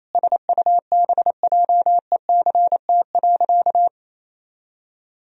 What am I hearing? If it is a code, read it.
Morse code, heard as a keyed tone, about 28 wpm, 700 Hz. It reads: SUBJECT.
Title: Sentences from Top 1000 Words - Review Part 4 - 28wpm